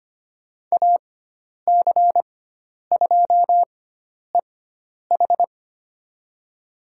Morse code, heard as a keyed tone, about 25 wpm, spaced as for 10 wpm, 700 Hz. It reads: AC2EH
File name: Lesson 16g - Letter c - Call Signs 25wpm